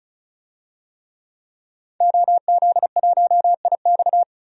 Morse code, heard as a keyed tone, about 36 wpm, 700 Hz. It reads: OZ1IX
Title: DXCC-100 Call Signs - 5 Characters - 35wpm